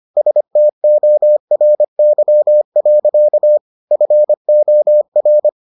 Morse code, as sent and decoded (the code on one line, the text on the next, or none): ... - --- .-. -.-- .-.-.- / ..-. --- .-.
STORY. FOR